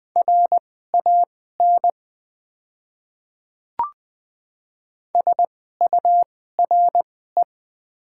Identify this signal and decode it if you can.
Morse code, heard as a keyed tone, about 20 wpm, spaced as for 14 wpm, 700 Hz. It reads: RAN SURE
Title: 2x Character Spacing - Top 500 Words - 20wpm